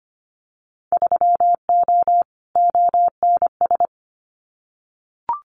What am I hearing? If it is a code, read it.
Morse code, heard as a keyed tone, about 25 wpm, 700 Hz. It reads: 3O ONS
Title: Winter Field Day Exchanges Encore 25wpm